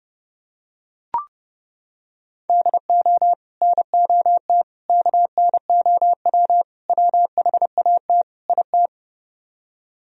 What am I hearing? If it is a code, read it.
Morse code, heard as a keyed tone, about 29 wpm, 700 Hz. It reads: DO NOT KNOW WHAT IT